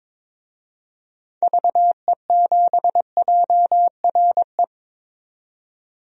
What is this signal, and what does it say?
Morse code, heard as a keyed tone, about 22 wpm, 700 Hz. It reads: VE7JRE
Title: DXCC-100 Call Signs - 6 Characters - 22wpm